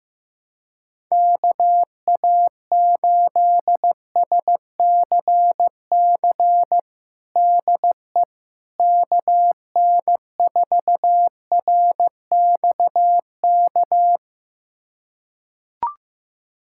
Morse code, heard as a keyed tone, about 15 wpm, 700 Hz. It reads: KA8SCC DE KN4RXK
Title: USA Callsign DE Callsign 15wpm